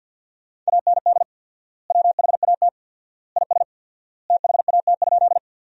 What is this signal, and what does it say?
Morse code, heard as a keyed tone, about 50 wpm, 700 Hz. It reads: AND WHAT IS THAT?
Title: 4x Word-Spacing - Sentences from Top 100 Words - 50wpm